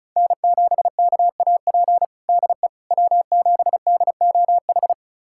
Morse code, heard as a keyed tone, about 36 wpm, 700 Hz. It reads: N7KAP DE W7DOH
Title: USA Callsign DE Callsign - 5 Characters - 35wpm